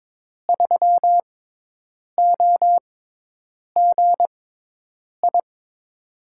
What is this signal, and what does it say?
Morse code, heard as a keyed tone, about 22 wpm, spaced as for 8 wpm, 700 Hz. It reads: 3OGI
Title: USA Call Signs - 6 Characters - 6x Character Spacing - 22wpm